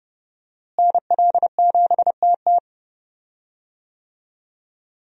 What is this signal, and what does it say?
Morse code, heard as a keyed tone, about 32 wpm, 700 Hz. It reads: NL7TT
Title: USA Call Signs - 5 Characters - Repeated 2x - 30wpm